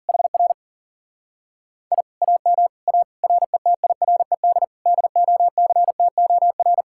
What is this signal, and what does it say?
Morse code, heard as a keyed tone, about 50 wpm, 700 Hz. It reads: HR I AM A RETIRED DOCTOR
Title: Speed-Racing Common QSO Phrases - 40wpm